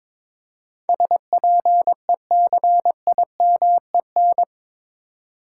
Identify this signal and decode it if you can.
Morse code, heard as a keyed tone, about 21 wpm, 700 Hz. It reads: SPECIMEN
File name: practice / 22wpm / Eight Letter Words - 22wpm